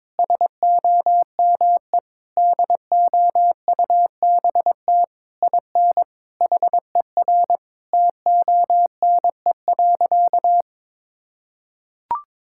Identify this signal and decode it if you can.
Morse code, heard as a keyed tone, about 22 wpm, 700 Hz. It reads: SOME DOUBT IN HER TONE.
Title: Sentences from Top 1600 Words 22wpm